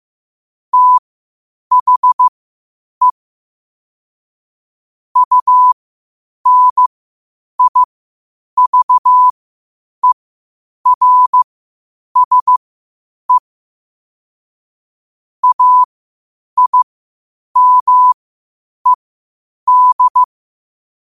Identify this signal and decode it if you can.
Morse code, heard as a keyed tone, about 15 wpm, spaced as for 8 wpm, 1000 Hz. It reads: THE UNIVERSE AIMED
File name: qotdfiles / 8wpm